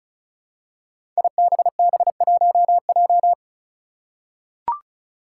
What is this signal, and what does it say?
Morse code, heard as a keyed tone, about 36 wpm, 700 Hz. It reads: IBB1J
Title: Five Challenging Letter-Number Combinations 35wpm